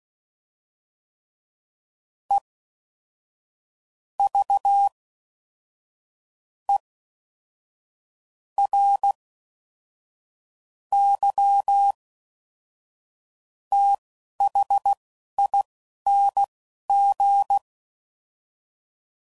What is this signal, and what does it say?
Morse code, heard as a keyed tone, about 16 wpm, spaced as for 11 wpm, 800 Hz. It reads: E V E R Y THING
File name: Suffixes / Special INGs / Fast INGs